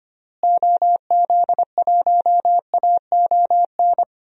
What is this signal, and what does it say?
Morse code, heard as a keyed tone, about 25 wpm, 700 Hz. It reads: OZ1AON
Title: DXCC-100 Call Signs - 6 Characters - Repeated 3x - 25wpm